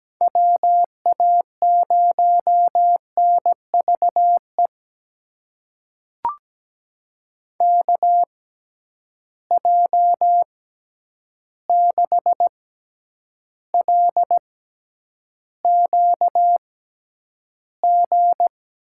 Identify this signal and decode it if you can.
Morse code, heard as a keyed tone, about 17 wpm, spaced as for 9 wpm, 700 Hz. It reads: WA0NVE KJ6LQG
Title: USA Call Signs - 6 Characters - 6x Character Spacing - 17wpm